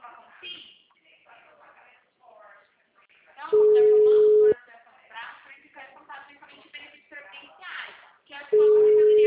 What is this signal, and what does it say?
Ringback tone (Europe, 425 Hz), with speech.
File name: out- -2010- - - .43871